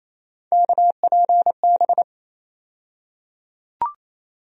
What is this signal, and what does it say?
Morse code, heard as a keyed tone, about 28 wpm, 700 Hz. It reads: KPB
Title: USA Call Sign Postfixes - 3 Letters 28wpm